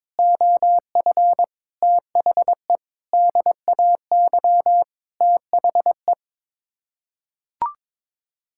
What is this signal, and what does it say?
Morse code, heard as a keyed tone, about 22 wpm, 700 Hz. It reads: OF THE DAY THE
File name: practice / 22wpm / Sets of 8 Words - 22wpm